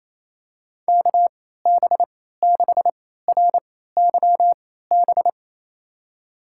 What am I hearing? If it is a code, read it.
Morse code, heard as a keyed tone, about 27 wpm, spaced as for 16 wpm, 700 Hz. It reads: KB6RYB
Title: USA Call Signs - 6 Characters - 3x Character Spacing - 28wpm